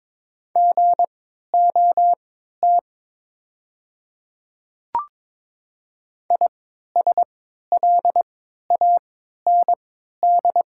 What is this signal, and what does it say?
Morse code, heard as a keyed tone, about 22 wpm, spaced as for 12 wpm, 700 Hz. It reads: GOT ISLAND
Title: 3x Character Spacing - Top 500 Words - 22wpm